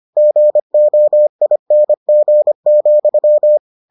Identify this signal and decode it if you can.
Morse code, heard as a keyed tone, about 25 wpm, 600 Hz. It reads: GOING,